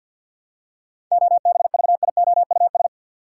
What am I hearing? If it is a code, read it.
Morse code, heard as a keyed tone, about 52 wpm, 700 Hz. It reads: OBVIOUS
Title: Seven Letter Words - 50wpm